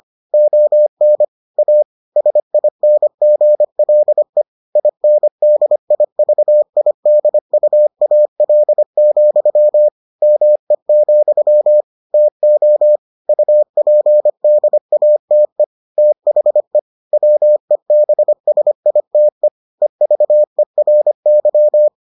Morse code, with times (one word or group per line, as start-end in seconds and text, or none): 0.3-1.2 ON
1.6-1.8 A
2.2-4.4 SINGLE
4.8-9.9 INDIVIDUAL,
10.2-11.8 ME,
12.1-13.0 TO
13.3-15.6 UPDATE
16.0-16.8 THE
17.1-19.5 WEBSITE
19.8-21.9 EVERY